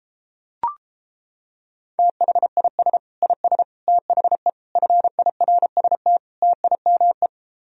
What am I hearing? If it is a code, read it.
Morse code, heard as a keyed tone, about 29 wpm, 700 Hz. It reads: THIS IS THE FIRST TIME